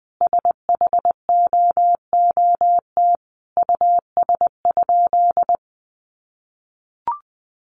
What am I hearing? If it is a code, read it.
Morse code, heard as a keyed tone, about 20 wpm, 700 Hz. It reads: SHOOT US?
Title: Sentences from Top 1200 Words 20wpm